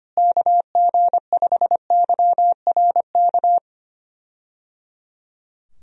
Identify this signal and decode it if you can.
Morse code, heard as a keyed tone, about 25 wpm, 700 Hz. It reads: KG5YRK
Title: USA Call Signs - 6 Characters - Repeated 3x - 25wpm